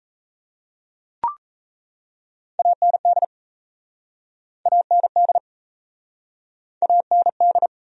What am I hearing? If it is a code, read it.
Morse code, heard as a keyed tone, about 37 wpm, 700 Hz. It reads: AND AND AND